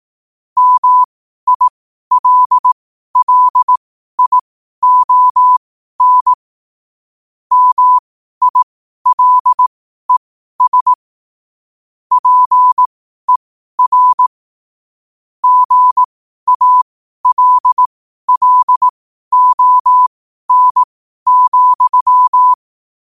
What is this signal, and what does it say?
Morse code, heard as a keyed tone, about 18 wpm, spaced as for 12 wpm, 1000 Hz. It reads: MILLION MILES PER GALLON,
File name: qotdfiles / 12wpm